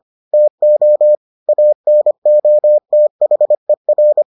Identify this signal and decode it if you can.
Morse code, heard as a keyed tone, about 25 wpm, 600 Hz. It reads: TO ANOTHER